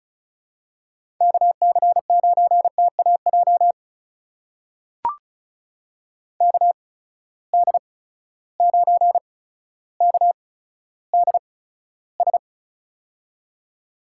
Morse code, heard as a keyed tone, about 36 wpm, spaced as for 15 wpm, 700 Hz. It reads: KC9TAJ KD9KDS
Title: USA Call Signs - 6 Characters - 8x Character Spacing - 35wpm